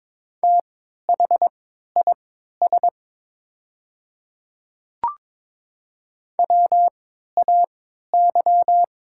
Morse code, heard as a keyed tone, about 22 wpm, spaced as for 13 wpm, 700 Hz. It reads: THIS WAY
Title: 3x Character Spacing - Top 100 Words - 22wpm